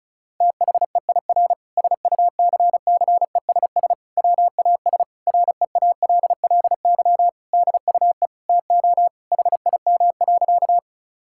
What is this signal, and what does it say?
Morse code, heard as a keyed tone, about 36 wpm, 700 Hz. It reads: THEIR SUCCESS WAS REALLY DUE TO HIM.
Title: Sentences from Top 1000 Words - Review Part 6 - 35wpm